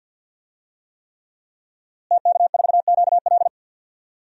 Morse code, heard as a keyed tone, about 47 wpm, 700 Hz. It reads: TK4QL
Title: DXCC-100 Call Signs - 5 Characters - Repeated 3x - 50wpm